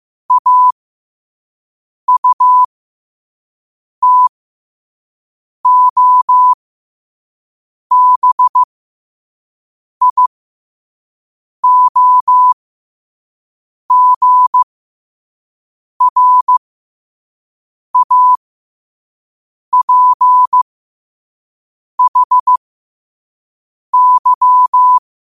Morse code, heard as a keyed tone, about 15 wpm, spaced as for 5 wpm, 1000 Hz. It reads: AUTOBIOGRAPHY
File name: qotdfiles / 5wpm